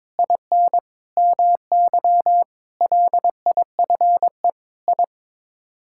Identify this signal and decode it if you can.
Morse code, heard as a keyed tone, about 22 wpm, 700 Hz. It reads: IN MY LIFE I
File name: practice / 22wpm / Sets of 8 Words - 22wpm